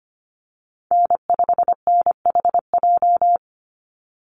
Morse code, heard as a keyed tone, about 25 wpm, 700 Hz. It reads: N5NHJ
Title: Call Signs - 25wpm